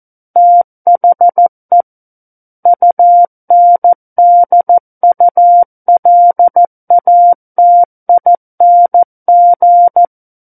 Morse code, heard as a keyed tone, about 14 wpm, 700 Hz. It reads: THE UNDULATING
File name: Book One / 05-Chapter 5 - 14 WPM